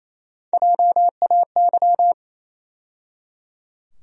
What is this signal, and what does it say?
Morse code, heard as a keyed tone, about 28 wpm, 700 Hz. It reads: JAY
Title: USA Call Sign Postfixes - 3 Letters 28wpm